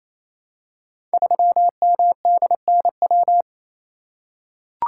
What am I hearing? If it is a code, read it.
Morse code, heard as a keyed tone, about 28 wpm, 700 Hz. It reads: 3MDNW
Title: Code Groups - 5 Letter:Numbers - 28wpm